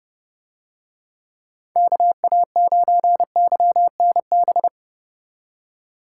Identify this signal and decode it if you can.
Morse code, heard as a keyed tone, about 28 wpm, 700 Hz. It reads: KA9YNB